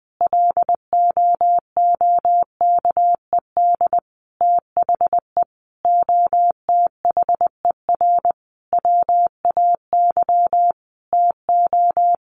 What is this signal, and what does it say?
Morse code, heard as a keyed tone, about 20 wpm, 700 Hz. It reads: LOOKED THE OTHER WAY TO
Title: Sentences from Top 800 Words 20wpm